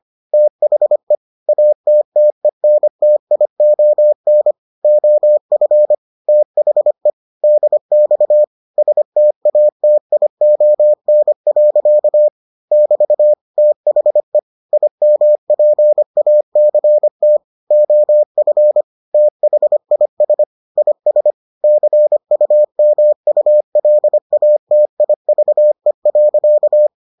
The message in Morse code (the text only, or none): THE ATTENTION OF THE DX STATION. <BT> THE IMPACT OF THIS IS CUMULATIVE.